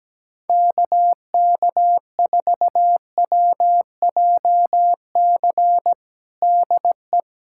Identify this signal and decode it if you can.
Morse code, heard as a keyed tone, about 17 wpm, 700 Hz. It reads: KK4WJC DE
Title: USA Callsign DE Callsign - 6 Characters - 17wpm